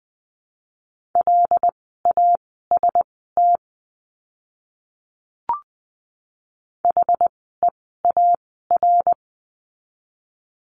Morse code, heard as a keyed tone, about 20 wpm, spaced as for 15 wpm, 700 Hz. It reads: LAST HEAR
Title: 206 Lesson 33f - Letter j - Words 20wpm